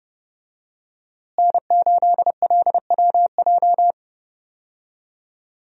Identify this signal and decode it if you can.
Morse code, heard as a keyed tone, about 30 wpm, 700 Hz. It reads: N8LWJ
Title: USA Call Signs - 5 Characters - Repeated 3x - 30wpm